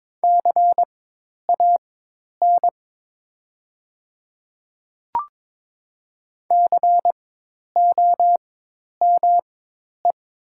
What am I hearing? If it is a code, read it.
Morse code, heard as a keyed tone, about 22 wpm, spaced as for 10 wpm, 700 Hz. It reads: CAN COME
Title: 4x Character Spacing - Top 100 Words - 22wpm